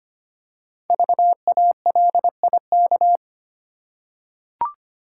Morse code, heard as a keyed tone, about 25 wpm, 700 Hz. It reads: VALIK
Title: Code Groups - 5 Letters - 25wpm